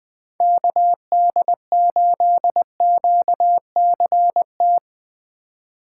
Morse code, heard as a keyed tone, about 20 wpm, 700 Hz. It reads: KD8QCT
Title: USA Call Signs - 6 Characters - 3x Character Spacing - 20wpm